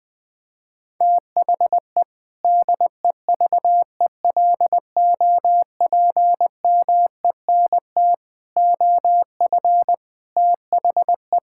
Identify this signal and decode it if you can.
Morse code, heard as a keyed tone, about 20 wpm, 700 Hz. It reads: THE DEVELOPMENT OF THE
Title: Sentences with 3-Word Phrases - 20wpm